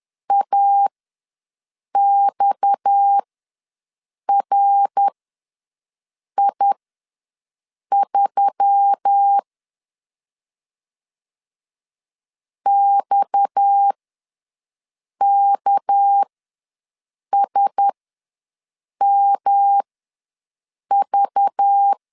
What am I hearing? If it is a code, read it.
Morse code, heard as a keyed tone, about 10 wpm, spaced as for 6 wpm, 800 Hz. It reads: AXRI3 XKSMV
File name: LEZIONI / 6 PKX0 / 6 PKX0 ws 10wpm 4220